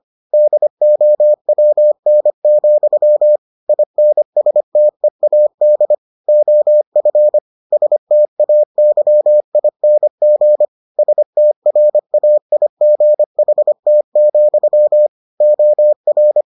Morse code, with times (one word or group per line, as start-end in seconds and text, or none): 0.3-3.4 DOWN,
3.7-5.9 INSTEAD
6.3-7.4 OF
7.7-10.6 STAYING
11.0-15.1 STRAIGHT,
15.4-16.4 OR